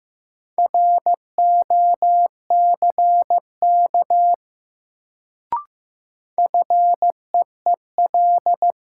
Morse code, heard as a keyed tone, about 15 wpm, 700 Hz. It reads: ROCK FEEL
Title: Sentences from Top 400 Words - New Word Drill - 15wpm